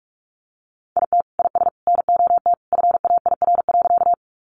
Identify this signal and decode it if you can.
Morse code, heard as a keyed tone, about 44 wpm, 700 Hz. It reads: IT IS NOT FAIR.
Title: Sentences from Top 1300 Words 45wpm